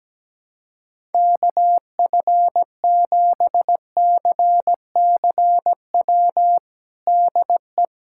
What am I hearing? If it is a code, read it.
Morse code, heard as a keyed tone, about 17 wpm, 700 Hz. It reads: KF7CCW DE
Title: USA Callsign DE Callsign - 6 Characters - 17wpm